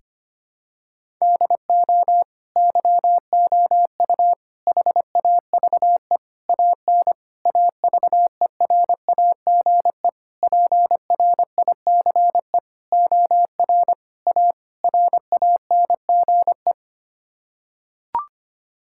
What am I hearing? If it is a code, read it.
Morse code, heard as a keyed tone, about 25 wpm, 700 Hz. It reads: DO YOU HAVE AN AVERAGE PRICE OR A RANGE
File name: Sets of 9 Words - 25wpm